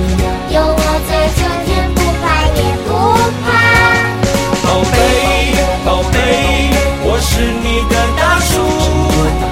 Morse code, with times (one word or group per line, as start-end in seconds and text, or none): none